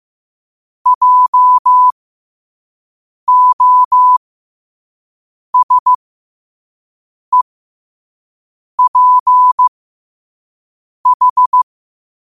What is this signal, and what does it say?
Morse code, heard as a keyed tone, about 15 wpm, spaced as for 5 wpm, 1000 Hz. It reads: JOSEPH
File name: 5wpm